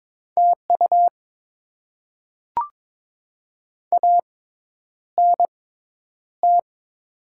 Morse code, heard as a keyed tone, about 22 wpm, spaced as for 10 wpm, 690 Hz. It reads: TU ANT
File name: Top 20 QSO Elements - 6x Character Spacing - 22wpm